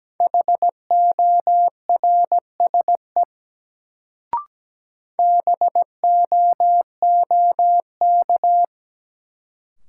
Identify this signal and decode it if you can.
Morse code, heard as a keyed tone, about 17 wpm, 700 Hz. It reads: HORSE BOOK
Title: Sentences from Top 300 Words - New Word Drill - 17wpm